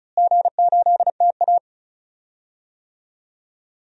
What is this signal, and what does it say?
Morse code, heard as a keyed tone, about 34 wpm, 700 Hz. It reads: G8TA
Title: DXCC-100 Call Signs - 4 Characters - Repeated 3x - 35wpm